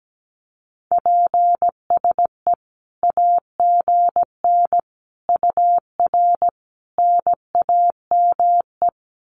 Morse code, heard as a keyed tone, about 17 wpm, 700 Hz. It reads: PSE AGN UR NAME
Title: Common QSO Phrases 17wpm